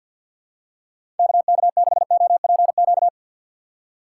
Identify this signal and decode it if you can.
Morse code, heard as a keyed tone, about 50 wpm, 700 Hz. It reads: KK6OPQ
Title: USA Call Signs - 6 Characters - 4x Character Spacing - 50wpm